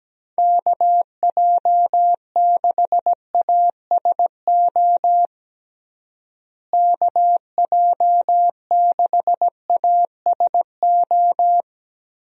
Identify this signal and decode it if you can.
Morse code, heard as a keyed tone, about 17 wpm, 700 Hz. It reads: KJ6ASO KJ6ASO
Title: USA Call Signs - 6 Characters - Repeated 3x - 17wpm